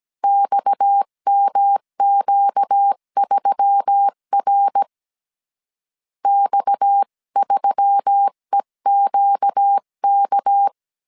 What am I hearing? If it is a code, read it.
Morse code, heard as a keyed tone, about 17 wpm, 800 Hz. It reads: XMQ3R X3EQK
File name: CW / DET / 1 6 1500C S C 16wpm 2733